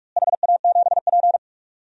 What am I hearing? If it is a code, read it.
Morse code, heard as a keyed tone, about 46 wpm, 700 Hz. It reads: HA7P
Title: DXCC-100 Call Signs - 4 Characters - Repeated 3x - 45wpm